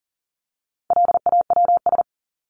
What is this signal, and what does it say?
Morse code, heard as a keyed tone, about 40 wpm, 700 Hz. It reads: LAWS
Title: ES:S Suffix - Top 100 Words with Base Words 40wpm